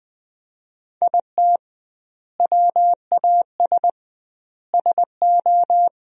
Morse code, heard as a keyed tone, about 20 wpm, 700 Hz. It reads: IT WAS SO
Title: 2x Word-Spacing - Sentences from Top 500 Words - 20wpm